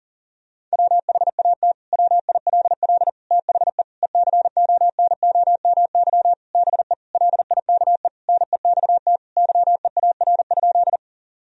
Morse code, heard as a keyed tone, about 40 wpm, 700 Hz. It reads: WHAT WILL THE ECONOMY BE LIKE NEXT YEAR?
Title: Sentences from Top 1000 Words - Review Part 3 - 40wpm